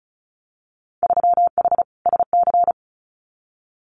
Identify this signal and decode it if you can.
Morse code, heard as a keyed tone, about 34 wpm, 700 Hz. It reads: 3H SC
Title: Winter Field Day Exchanges 35wpm